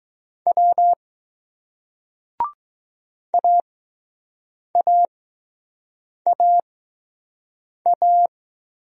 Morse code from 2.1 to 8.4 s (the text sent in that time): AAAA